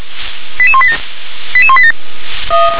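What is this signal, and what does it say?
Five-tone selective call (ZVEI), heard as quick runs of five tones, tones of about 70 ms. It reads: 88187 88187
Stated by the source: ZVEI five-tone call received on SDR.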